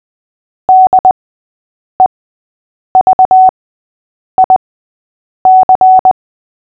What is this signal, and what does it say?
Morse code, heard as a keyed tone, about 20 wpm, spaced as for 8 wpm, 750 Hz. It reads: DEVIC
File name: cw / fists / Short-Story-6-8wpm